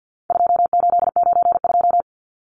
Morse code, heard as a keyed tone, about 50 wpm, 700 Hz. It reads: 2892
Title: Numbers - 4-Digit 50wpm